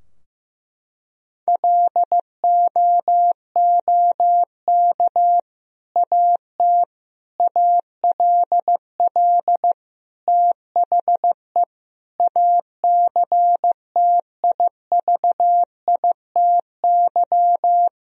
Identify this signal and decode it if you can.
Morse code, heard as a keyed tone, about 16 wpm, 700 Hz. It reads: LOOK AT ALL THE ACTIVITY
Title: Sentences from Top 1000 Words - Review Part 4 - 15wpm